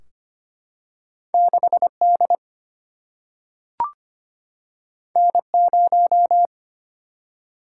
Morse code, heard as a keyed tone, about 25 wpm, 700 Hz. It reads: 6D N0
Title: Single-Letter Single-Number 25wpm